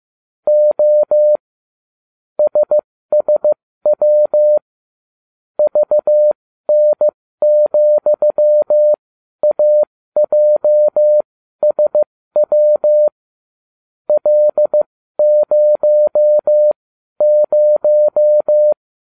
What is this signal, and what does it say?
Morse code, heard as a keyed tone, about 15 wpm, spaced as for 12 wpm, 600 Hz. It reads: O SSW VN,AJSW L00
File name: cw / ubacw20 3